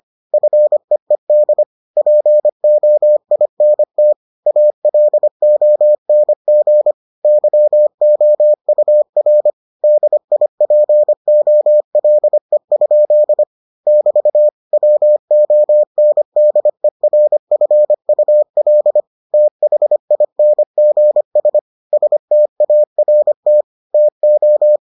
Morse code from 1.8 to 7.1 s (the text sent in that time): POINT ALONG